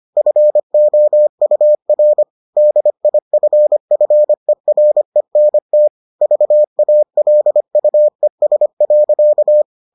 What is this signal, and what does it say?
Morse code, heard as a keyed tone, about 25 wpm, 600 Hz. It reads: FOUR DIFFERENT VALUES.